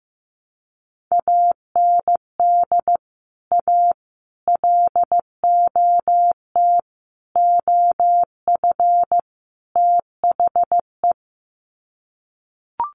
Morse code, heard as a keyed tone, about 15 wpm, 700 Hz. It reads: AND A LOT OF THE
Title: Sets of 5 Words Encore 15wpm